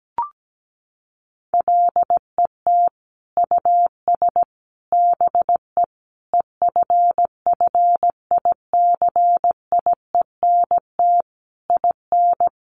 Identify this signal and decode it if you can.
Morse code, heard as a keyed tone, about 17 wpm, 700 Hz. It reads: LET US BE EFFICIENT IN
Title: Sentences from Top 1300 Words 17wpm